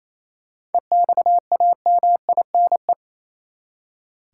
Morse code, heard as a keyed tone, about 28 wpm, 700 Hz. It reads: EXAMINE
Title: Seven Letter Words - 28wpm